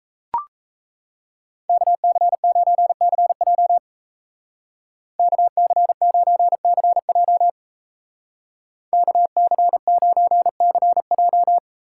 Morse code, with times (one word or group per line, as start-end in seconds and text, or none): 1.7-3.8 KC9CJ
5.2-7.5 KC9CJ
8.9-11.6 KC9CJ